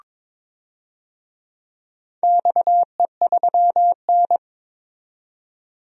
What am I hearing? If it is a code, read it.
Morse code, heard as a keyed tone, about 22 wpm, 700 Hz. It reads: XE3N
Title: DXCC-100 Call Signs - 4 Characters - 22wpm